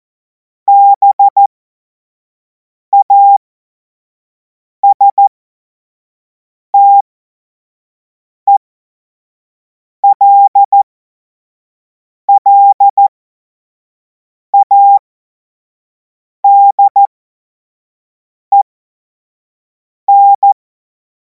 Morse code, heard as a keyed tone, about 14 wpm, spaced as for 5 wpm, 800 Hz. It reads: BASTELLADEN
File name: Morsekurs / Witze / Witz 28. April 2026 25 F70